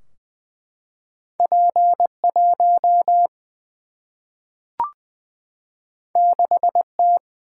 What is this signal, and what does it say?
Morse code, heard as a keyed tone, about 20 wpm, 700 Hz. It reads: P 1 6 T
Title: Single-Letter Single-Number 20wpm